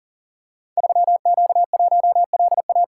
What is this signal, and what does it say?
Morse code, heard as a keyed tone, about 43 wpm, 700 Hz. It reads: 3Q1LA
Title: Code Groups - 5 Letter:Numbers - 40wpm